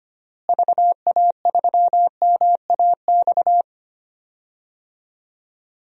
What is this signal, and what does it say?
Morse code, heard as a keyed tone, about 25 wpm, 700 Hz. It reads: VA3MAX
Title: DXCC-100 Call Signs - 6 Characters - Repeated 2x - 25wpm